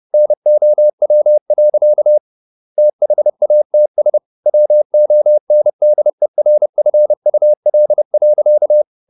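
Morse code, heard as a keyed tone, about 30 wpm, 600 Hz. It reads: NOW. THATS WONDERFUL.
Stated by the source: Morse code at 30 wpm, the novel Never by Ken Follett in English, from CW med Gnister.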